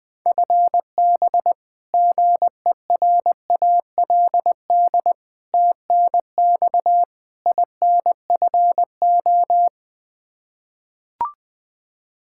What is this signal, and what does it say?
Morse code, heard as a keyed tone, about 21 wpm, 700 Hz. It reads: FB GERALD TNX INFO
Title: Practice QSO - Exchange 3 - More Info - 20wpm